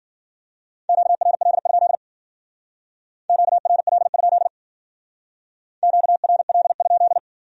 Morse code, heard as a keyed tone, about 52 wpm, 700 Hz. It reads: QRL? QRL? QRL?